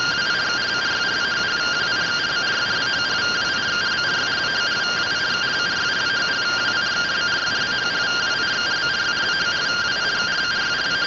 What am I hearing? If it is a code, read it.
Two-tone teleprinter signal, about 170 Hz shift.
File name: findings / AXIAL fr45 3mm.gen-23 rtty rx